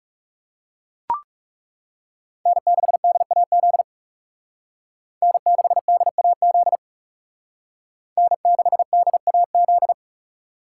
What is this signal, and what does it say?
Morse code, heard as a keyed tone, about 35 wpm, 700 Hz. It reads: N6DAZ N6DAZ N6DAZ
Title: Speed Racing - USA 1x3 Call Signs 30wpm